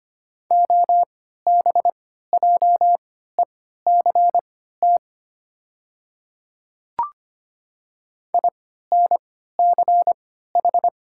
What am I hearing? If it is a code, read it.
Morse code, heard as a keyed tone, about 25 wpm, spaced as for 14 wpm, 700 Hz. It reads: OBJECT INCH
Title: 3x Character Spacing - Top 500 Words - 25wpm